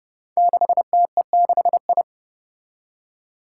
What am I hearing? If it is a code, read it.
Morse code, heard as a keyed tone, about 30 wpm, 700 Hz. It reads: 6TE6I
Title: Five Challenging Letter-Number Combinations 30wpm